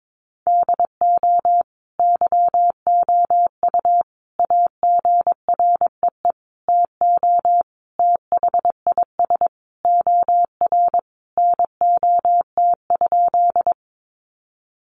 Morse code, heard as a keyed tone, about 22 wpm, 700 Hz. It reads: DO YOU AGREE TO THIS OR NOT?
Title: Sentences from Top 1100 Words 22wpm